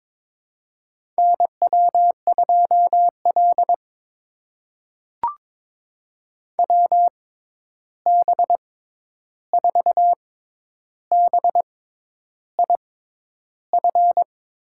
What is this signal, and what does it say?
Morse code, heard as a keyed tone, about 22 wpm, spaced as for 10 wpm, 700 Hz. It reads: NW2L WB4BIF
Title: USA Call Signs - 6x Character Spacing - 22wpm